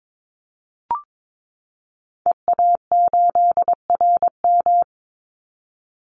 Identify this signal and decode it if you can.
Morse code, heard as a keyed tone, about 22 wpm, 700 Hz. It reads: EA8RM
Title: Call Signs Repeated 2x - 22wpm